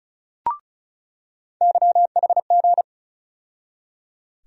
Morse code, heard as a keyed tone, about 36 wpm, 700 Hz. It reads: YHG
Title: USA Call Sign Postfixes - 3 Letters 35wpm